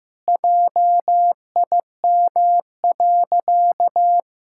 Morse code, heard as a keyed tone, about 15 wpm, 700 Hz. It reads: JIM.
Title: Pangram Challenge v2 - 15wpm